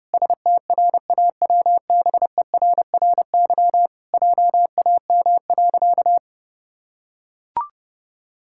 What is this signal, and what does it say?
Morse code, heard as a keyed tone, about 30 wpm, 700 Hz. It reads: STRAWBERRY JAM.
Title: Pangram Challenge v2 - 30wpm